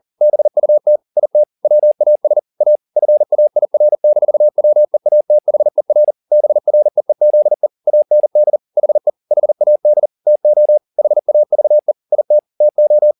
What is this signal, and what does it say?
Morse code, heard as a keyed tone, about 40 wpm, 600 Hz. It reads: BUT IT WAS A FAIR-WEATHER BREEZE AND HE HAD TO HAVE IT TO
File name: the old man and the sea 2 40wpm